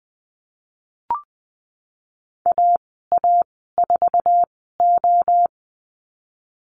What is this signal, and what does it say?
Morse code, heard as a keyed tone, about 20 wpm, spaced as for 15 wpm, 700 Hz. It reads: AA4O
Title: 048 Lesson 9g - Number 4 - Call Signs 20wpm